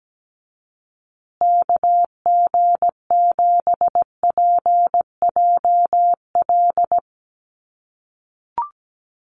Morse code, heard as a keyed tone, about 17 wpm, 700 Hz. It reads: KG7PJL
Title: USA Call Signs Repeated 3x 17wpm